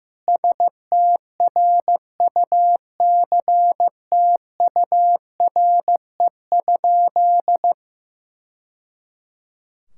Morse code, heard as a keyed tone, about 15 wpm, 700 Hz. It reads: STRUCTURE?
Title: Sentences from Top 1000 Words - Review Part 6 - 15wpm